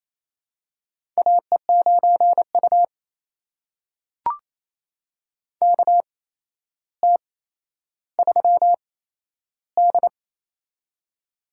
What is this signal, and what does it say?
Morse code, heard as a keyed tone, about 28 wpm, spaced as for 12 wpm, 700 Hz. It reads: AE9U KT3D
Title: USA Call Signs - 4 Characters - 8x Character Spacing - 28wpm